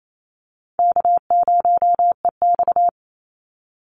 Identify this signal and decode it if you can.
Morse code, heard as a keyed tone, about 28 wpm, 700 Hz. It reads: K0EX
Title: Call Signs - 28wpm